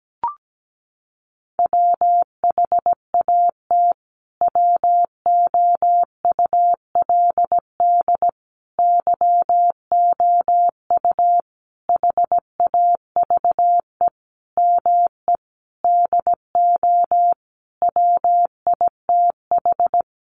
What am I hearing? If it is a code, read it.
Morse code, heard as a keyed tone, about 17 wpm, 700 Hz. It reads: WHAT WOULD YOU HAVE ME DO WITH
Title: Sentences from Top 300 Words 17wpm